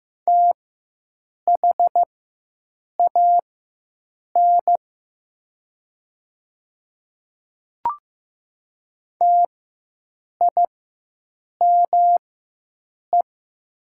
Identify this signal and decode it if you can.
Morse code, heard as a keyed tone, about 15 wpm, spaced as for 7 wpm, 700 Hz. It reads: THAN TIME
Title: 4x Character Spacing - Top 100 Words - 15wpm